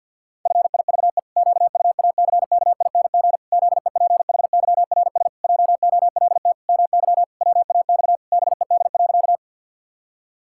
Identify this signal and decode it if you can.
Morse code, heard as a keyed tone, about 50 wpm, 700 Hz. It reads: FIVE QUACKING ZEPHYRS JOLT MY WAX BED.
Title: Pangram Challenge v2 - 50wpm